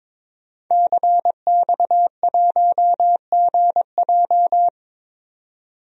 Morse code, heard as a keyed tone, about 21 wpm, 700 Hz. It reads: CX1GJ
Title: DXCC-100 Call Signs - 5 Characters - 22wpm